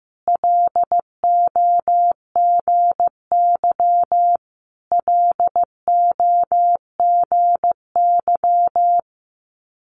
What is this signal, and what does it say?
Morse code, heard as a keyed tone, about 15 wpm, 700 Hz. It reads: LOGY LOGY
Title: Multi-letter Phonemes Repeated 3x 15wpm